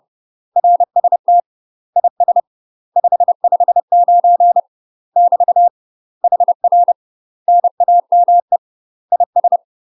Morse code, heard as a keyed tone, about 30 wpm, 700 Hz. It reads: RST IS 559 <BT> HR NAME IS
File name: UK MASTER QSOs TRAINING / 30WPM / UK MASTER QSOs 0007